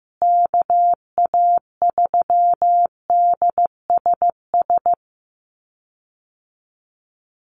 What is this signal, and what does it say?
Morse code, heard as a keyed tone, about 15 wpm, 700 Hz. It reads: KA3DSS
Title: USA Call Signs Repeated 3x 15wpm